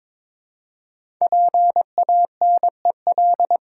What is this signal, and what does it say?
Morse code, heard as a keyed tone, about 22 wpm, 700 Hz. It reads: PANEL
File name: 5 Letter Words - First 3 Letters Given - 22wpm